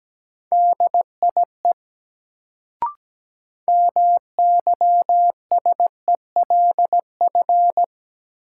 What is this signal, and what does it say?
Morse code, heard as a keyed tone, about 17 wpm, 700 Hz. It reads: DIE MYSELF
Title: Sentences from Top 1300 Words - New Word Drill Alt - 17wpm